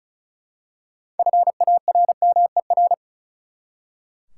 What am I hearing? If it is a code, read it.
Morse code, heard as a keyed tone, about 34 wpm, 700 Hz. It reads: FARMER
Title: Six Letter Words - 35wpm